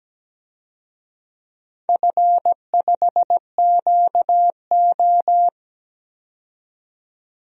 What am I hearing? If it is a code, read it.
Morse code, heard as a keyed tone, about 17 wpm, 700 Hz. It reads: F5QO
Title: DXCC-100 Call Signs - 4 Characters - 17wpm